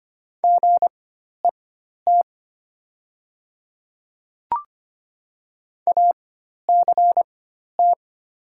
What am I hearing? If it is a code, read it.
Morse code, heard as a keyed tone, about 25 wpm, spaced as for 12 wpm, 700 Hz. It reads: GET ACT
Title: 4x Character Spacing - Top 500 Words - 25wpm